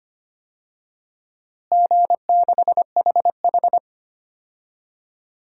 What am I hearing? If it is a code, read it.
Morse code, heard as a keyed tone, about 25 wpm, 700 Hz. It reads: G6HH
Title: DXCC-100 Call Signs - 4 Characters - Repeated 3x - 25wpm